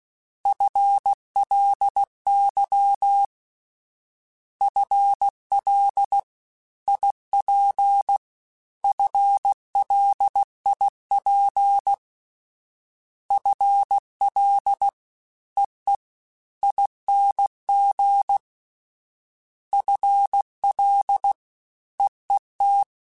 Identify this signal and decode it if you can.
Morse code, heard as a keyed tone, about 16 wpm, 800 Hz. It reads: FLY FL IP FLIP FL EE ING FL EET
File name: FL / FL mp3s / Fast FL @15